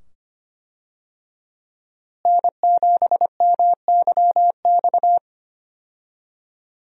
Morse code, heard as a keyed tone, about 25 wpm, 700 Hz. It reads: N7MYX
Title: USA Call Signs - 5 Characters - Repeated 2x - 25wpm